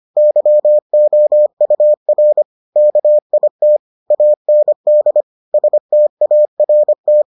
Morse code, heard as a keyed tone, about 25 wpm, 600 Hz. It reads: YOUR KIT AND START